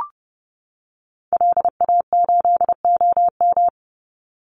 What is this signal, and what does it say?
Morse code, heard as a keyed tone, about 30 wpm, 700 Hz. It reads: LA8OM
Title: Call Signs - 30wpm